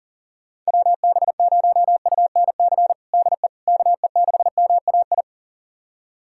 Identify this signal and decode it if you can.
Morse code, heard as a keyed tone, about 37 wpm, 700 Hz. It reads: WB0UNC DE KE6MAI
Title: USA Callsign DE Callsign 40wpm